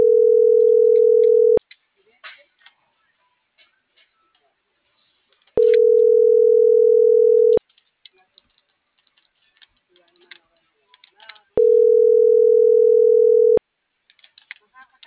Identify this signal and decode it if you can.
Ringback tone (North America), 440 + 480 Hz.